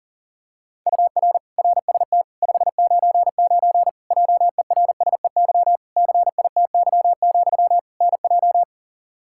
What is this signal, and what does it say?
Morse code, heard as a keyed tone, about 40 wpm, 700 Hz. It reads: UR RST 599 JERSEY CITY, NJ
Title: Practice QSO - Exchange 1 - Beginning - 40wpm